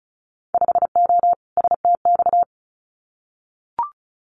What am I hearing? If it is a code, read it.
Morse code, heard as a keyed tone, about 34 wpm, 690 Hz. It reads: 5O STX
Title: Winter Field Day Exchanges 35wpm